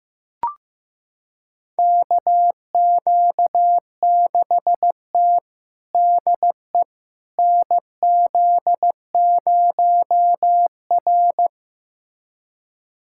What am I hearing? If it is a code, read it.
Morse code, heard as a keyed tone, about 15 wpm, 700 Hz. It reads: KQ6T DE NZ0R
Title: USA Callsign DE Callsign - 4 Characters - 15wpm